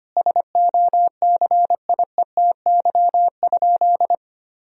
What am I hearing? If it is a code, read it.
Morse code, heard as a keyed tone, about 25 wpm, 700 Hz. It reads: SOCIETY?
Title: Sentences from Top 1000 Words - Review Part 4 - 25wpm